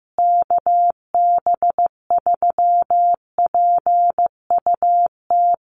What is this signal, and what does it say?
Morse code, heard as a keyed tone, about 15 wpm, 700 Hz. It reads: KB3PUT
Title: USA Call Signs Repeated 3x 15wpm